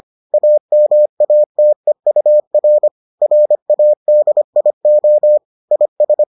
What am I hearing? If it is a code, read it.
Morse code, heard as a keyed tone, about 25 wpm, 600 Hz. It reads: AMATEUR RADIO IS